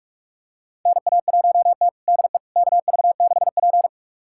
Morse code, heard as a keyed tone, about 46 wpm, 700 Hz. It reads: NA1T DE KV6P
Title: USA Callsign DE Callsign - 4 Characters - 45wpm